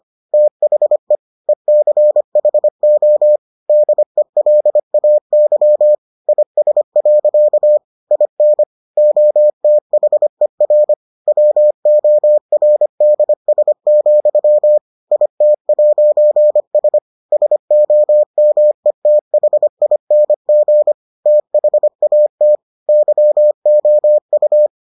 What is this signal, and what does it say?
Morse code, heard as a keyed tone, about 25 wpm, 600 Hz. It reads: THE ECHO DELAY IS. IN OTHER WORDS, IT'S SOMETHING THAT YOU